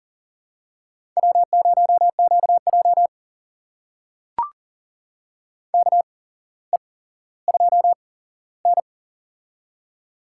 Morse code, heard as a keyed tone, about 40 wpm, spaced as for 17 wpm, 700 Hz. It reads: W0QJ KE2N
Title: USA Call Signs - 4 Characters - 8x Character Spacing - 40wpm